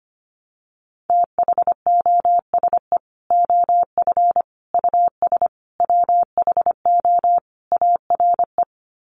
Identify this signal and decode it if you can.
Morse code, heard as a keyed tone, about 25 wpm, 700 Hz. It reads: THOSE OF US WHO ARE
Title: Sets of 5 Words Encore2 25wpm